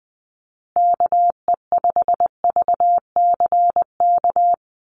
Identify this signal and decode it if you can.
Morse code, heard as a keyed tone, about 20 wpm, 700 Hz. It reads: KE5VCK
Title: USA Call Signs Repeated 2x 20wpm